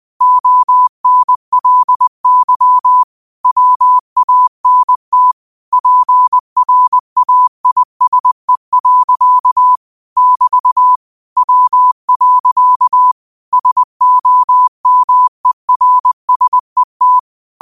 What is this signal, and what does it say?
Morse code, heard as a keyed tone, about 20 wpm, 1000 Hz. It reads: ONLY WANT PRAISE. <BT> W. SOMERSET